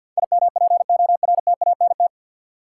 Morse code, heard as a keyed tone, about 48 wpm, 700 Hz. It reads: IMPORTANT
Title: Sets of 3 Words - New Word Drill - 50wpm